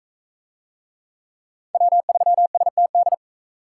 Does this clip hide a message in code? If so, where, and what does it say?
Yes: Morse code from 1.7 to 3.1 s, heard as a keyed tone, about 40 wpm, 700 Hz. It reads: W3STD